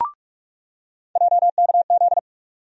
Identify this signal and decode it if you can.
Morse code, heard as a keyed tone, about 46 wpm, 700 Hz. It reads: JKZ
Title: USA Call Sign Postfixes - 3 Letters 45wpm